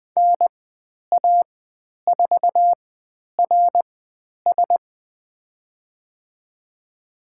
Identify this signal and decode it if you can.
Morse code, heard as a keyed tone, about 20 wpm, spaced as for 10 wpm, 700 Hz. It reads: NA4RS
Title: 102 Lesson 17g - Letter u - Call Signs 20wpm